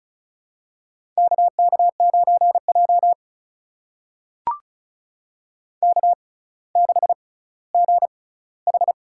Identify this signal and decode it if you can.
Morse code, heard as a keyed tone, about 36 wpm, spaced as for 18 wpm, 700 Hz. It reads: KK9J K6GH
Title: USA Call Signs - 4 Characters - 6x Character Spacing - 35wpm